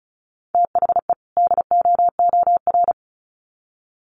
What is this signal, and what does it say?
Morse code, heard as a keyed tone, about 34 wpm, 700 Hz. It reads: THE DOOR
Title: Sets of 2 Words 35wpm